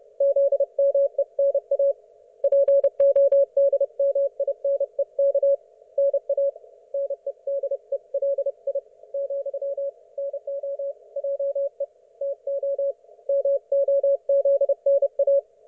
Morse code, heard as a keyed tone, about 30 wpm, 560 Hz. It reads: ZMENA PODMINEK NA NEDELI , NO JE TO MOZNA
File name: Archiv / CUC179 / ok1fhi cul179